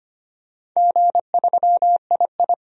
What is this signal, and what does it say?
Morse code, heard as a keyed tone, about 25 wpm, 700 Hz. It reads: G3II
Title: DXCC-100 Call Signs - 4 Characters - 25wpm